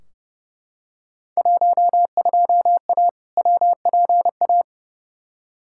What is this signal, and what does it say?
Morse code, heard as a keyed tone, about 30 wpm, 700 Hz. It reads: 12A WPA
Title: Field Day Exchanges Encore - 30wpm